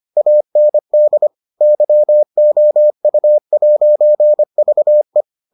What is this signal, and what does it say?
Morse code, heard as a keyed tone, about 25 wpm, 600 Hz. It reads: AND YOU'VE